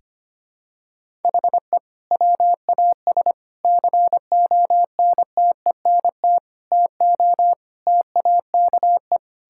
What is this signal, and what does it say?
Morse code, heard as a keyed tone, about 25 wpm, 700 Hz. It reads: HE WAS CONTENT TO TAKE
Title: Sentences from Top 1000 Words - Review Part 6 - 25wpm